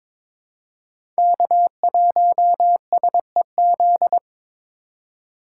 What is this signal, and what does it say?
Morse code, heard as a keyed tone, about 22 wpm, 700 Hz. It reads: K1SEZ
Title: USA Call Signs - 5 Characters - 6x Character Spacing - 22wpm